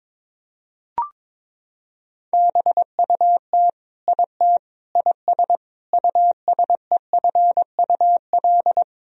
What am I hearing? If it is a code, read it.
Morse code, heard as a keyed tone, about 22 wpm, 700 Hz. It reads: BUT IT IS USEFUL
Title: Sentences from Top 1000 Words - Review Part 4 - 22wpm